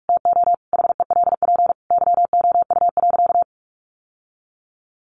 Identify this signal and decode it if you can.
Morse code, heard as a keyed tone, about 44 wpm, 700 Hz. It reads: TO HELP YOU.
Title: Sentences from Top 1200 Words 45wpm